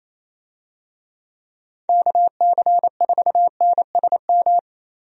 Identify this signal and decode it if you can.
Morse code, heard as a keyed tone, about 28 wpm, 700 Hz. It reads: KC4NSM
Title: USA Call Signs - 6 Characters - 28wpm